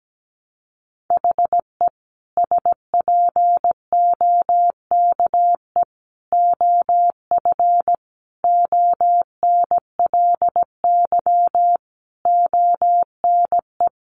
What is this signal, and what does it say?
Morse code, heard as a keyed tone, about 17 wpm, 700 Hz. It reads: HE SPOKE OF ONLY ONE
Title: Sentences from Top 1200 Words 17wpm